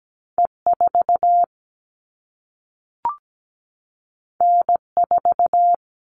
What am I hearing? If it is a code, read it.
Morse code, heard as a keyed tone, about 17 wpm, 700 Hz. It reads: E4 N4
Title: All Cut Numbers - 2-Digit 17wpm